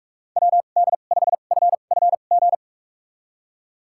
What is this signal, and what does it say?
Morse code, heard as a keyed tone, about 44 wpm, spaced as for 32 wpm, 700 Hz. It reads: WD5FFG
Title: USA Call Signs - 6 Characters - 2x Character Spacing - 45wpm